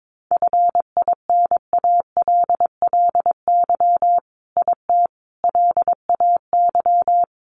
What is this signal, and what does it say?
Morse code, heard as a keyed tone, about 22 wpm, 700 Hz. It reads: FINALLY IT LAY
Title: Sentences from Top 1000 Words 22wpm